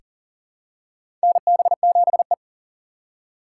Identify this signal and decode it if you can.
Morse code, heard as a keyed tone, about 40 wpm, 700 Hz. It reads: NB7E
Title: USA Call Signs - 4 Characters - 4x Character Spacing - 40wpm